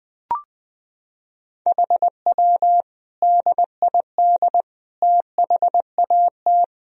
Morse code, heard as a keyed tone, about 21 wpm, 700 Hz. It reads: HW DID THAT
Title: Sentences from Top 200 Words and Abbreviations 20wpm